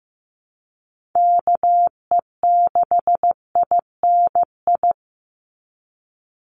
Morse code, heard as a keyed tone, about 15 wpm, 700 Hz. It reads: KE6INI
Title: USA Call Signs 15wpm